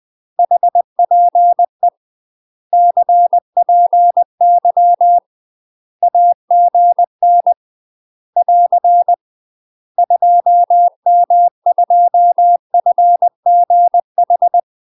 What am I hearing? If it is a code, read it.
Morse code, heard as a keyed tone, about 20 wpm, 700 Hz. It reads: HPE CPY AGN <AR> 2M2FGH
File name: UK MASTER QSOs TRAINING / 20WPM / UK MASTER QSOs 0045